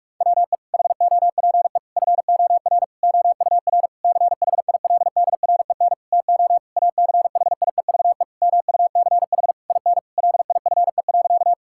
Morse code, heard as a keyed tone, about 46 wpm, 700 Hz. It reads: WE HOPE FOR OUR CHILDREN TO ACHIEVE MUCH IN LIFE.
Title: Sentences from Top 1000 Words - Review Part 6 - 45wpm